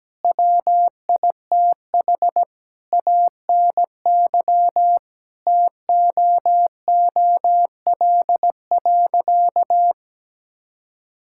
Morse code, heard as a keyed tone, about 17 wpm, 700 Hz. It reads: WITH ANY TOOL.
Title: Sentences from Top 1000 Words - Review Part 6 - 17wpm